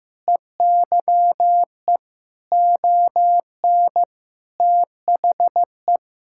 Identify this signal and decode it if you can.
Morse code, heard as a keyed tone, about 16 wpm, 700 Hz. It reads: EYE ON THE
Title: Sentences from Top 1000 Words - Review Part 5 - 15wpm